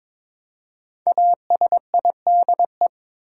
Morse code, heard as a keyed tone, about 22 wpm, 700 Hz. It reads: ASIDE
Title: Five Letter Words - 22wpm